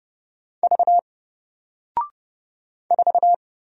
Morse code, heard as a keyed tone, about 30 wpm, 700 Hz. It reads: V 4
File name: Single-Letter-Number-30wpm